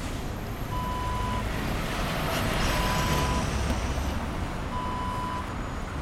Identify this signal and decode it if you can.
There is no code.